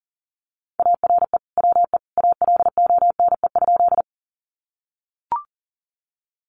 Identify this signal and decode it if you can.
Morse code, heard as a keyed tone, about 40 wpm, 700 Hz. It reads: ARE WE ALONE?
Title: Sentences from Top 1000 Words 40wpm